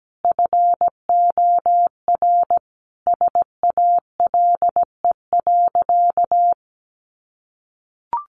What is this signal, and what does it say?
Morse code, heard as a keyed tone, about 17 wpm, 700 Hz. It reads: FOR SALE.
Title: Sentences from Top 1400 Words 17wpm